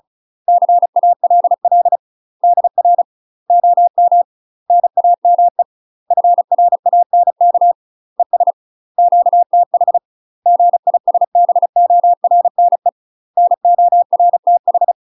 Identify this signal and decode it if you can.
Morse code, heard as a keyed tone, about 36 wpm, 700 Hz. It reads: CALL DR OM NAME FRANK ES QTH GISBORNE NORTH
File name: UK MASTER QSOs TRAINING / 35WPM / UK MASTER QSOs 0016